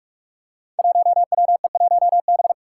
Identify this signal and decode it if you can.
Morse code, heard as a keyed tone, about 44 wpm, 700 Hz. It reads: 1WE1B
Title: Five Challenging Letter-Number Combinations 45wpm